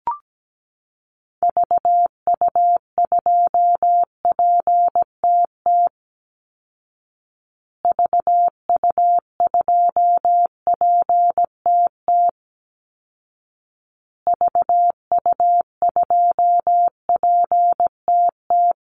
Morse code, heard as a keyed tone, about 17 wpm, 700 Hz. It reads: VU2PTT VU2PTT VU2PTT
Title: Call Signs Repeated 2x - 17wpm